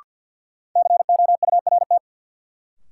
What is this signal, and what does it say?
Morse code, heard as a keyed tone, about 52 wpm, 700 Hz. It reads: COURT
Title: Sentences from Top 1600 Words - New Word Drill - 50wpm